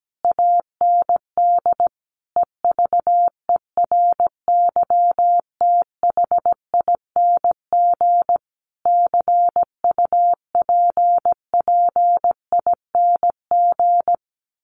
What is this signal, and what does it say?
Morse code, heard as a keyed tone, about 17 wpm, 700 Hz. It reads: AND EVERYTHING CUPPING
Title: A Sound Of Thunder 17wpm